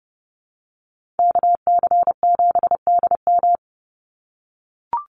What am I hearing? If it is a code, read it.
Morse code, heard as a keyed tone, about 30 wpm, 700 Hz. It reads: KC7DM
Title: NA SOTA Activator Callsigns 30wpm